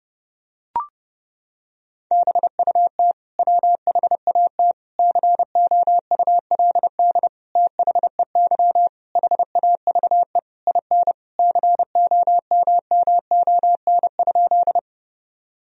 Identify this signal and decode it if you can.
Morse code, heard as a keyed tone, about 32 wpm, 700 Hz. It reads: BUT WHAT COULD THEY HAVE IN COMMON?
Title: Sentences from Top 1000 Words - Review Part 3 - 30wpm